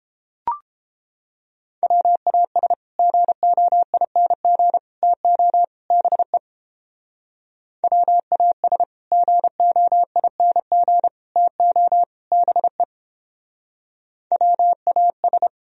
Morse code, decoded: WAS GOING TO BE WAS GOING TO BE WAS

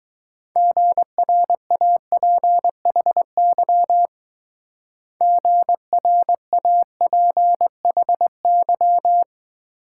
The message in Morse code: GRAPHY GRAPHY

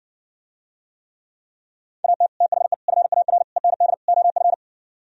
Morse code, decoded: AT THE FAR END OF

